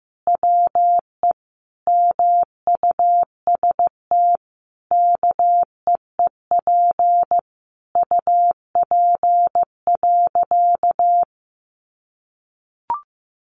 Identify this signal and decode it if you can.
Morse code, heard as a keyed tone, about 15 wpm, 700 Hz. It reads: WE MUST KEEP UP.
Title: Sentences from Top 700 Words 15wpm